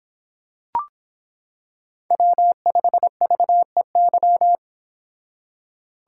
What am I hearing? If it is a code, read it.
Morse code, heard as a keyed tone, about 26 wpm, 700 Hz. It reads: W5VEY